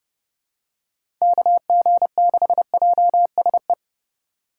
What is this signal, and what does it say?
Morse code, heard as a keyed tone, about 30 wpm, 700 Hz. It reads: KG6JSE
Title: USA Call Signs - 6 Characters - 2x Character Spacing - 30wpm